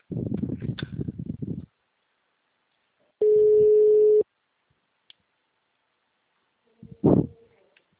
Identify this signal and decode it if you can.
Ringback tone (Europe, 425 Hz).